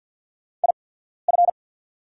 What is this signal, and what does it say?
Morse code, heard as a keyed tone, about 48 wpm, spaced as for 14 wpm, 690 Hz. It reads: IF
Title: 8x Character Spacing - Top 500 Words - 50wpm